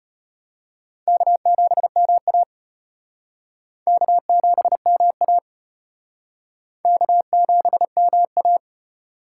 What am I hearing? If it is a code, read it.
Morse code, heard as a keyed tone, about 30 wpm, 700 Hz. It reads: K7MA K7MA K7MA
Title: Speed Racing - USA Call Signs 25wpm